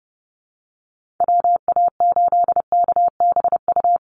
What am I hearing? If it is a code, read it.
Morse code, heard as a keyed tone, about 30 wpm, 700 Hz. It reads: WA8KBU
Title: Call Signs Repeated 2x - 30wpm